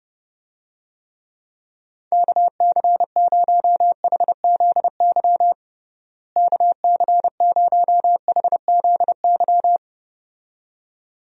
Morse code, heard as a keyed tone, about 30 wpm, 700 Hz. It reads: KC0HZY KC0HZY
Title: USA Call Signs - 6 Characters - Repeated 2x - 30wpm